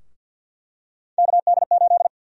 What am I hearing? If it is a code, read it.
Morse code, heard as a keyed tone, about 51 wpm, 700 Hz. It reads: KD8
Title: USA Call Sign Prefixes - Repeated 2x - 50wpm